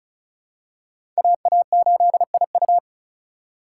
Morse code, heard as a keyed tone, about 34 wpm, 700 Hz. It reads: AA8IU
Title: USA Call Signs - 5 Characters - 4x Character Spacing - 35wpm